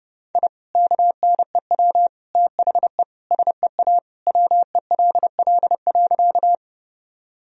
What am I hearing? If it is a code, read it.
Morse code, heard as a keyed tone, about 30 wpm, 700 Hz. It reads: I KNEW THE SEA WELL.
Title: Sentences from Top 1000 Words - Review Part 1 - 30wpm